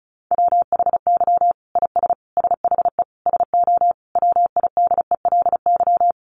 Morse code, heard as a keyed tone, about 34 wpm, 700 Hz. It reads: WHY IS SHE SO WIDELY
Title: Sentences from Top 1200 Words 35wpm